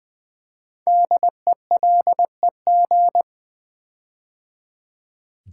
Morse code, heard as a keyed tone, about 20 wpm, 700 Hz. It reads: DELEG
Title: Build A Word - 5 Letters At A Time - 10-Letter Words - 20wpm